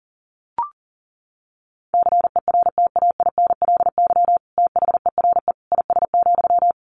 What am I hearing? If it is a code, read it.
Morse code, heard as a keyed tone, about 40 wpm, 700 Hz. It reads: CERTAINLY THERE IS,
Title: Sentences from Top 1100 Words 40wpm